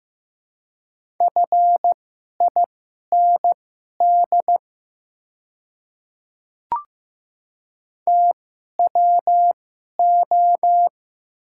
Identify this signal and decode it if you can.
Morse code, heard as a keyed tone, about 15 wpm, spaced as for 11 wpm, 700 Hz. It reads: FIND TWO
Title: 2x Character Spacing - Top 100 Words - 15wpm